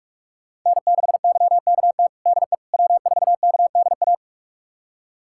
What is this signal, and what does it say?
Morse code, heard as a keyed tone, about 43 wpm, 700 Hz. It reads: N6YKT DE W4KDA
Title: USA Callsign DE Callsign - 5 Characters - 45wpm